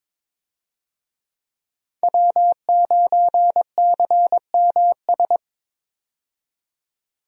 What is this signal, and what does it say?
Morse code, heard as a keyed tone, about 22 wpm, 700 Hz. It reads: W9CMS
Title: USA Call Signs - 5 Characters - 22wpm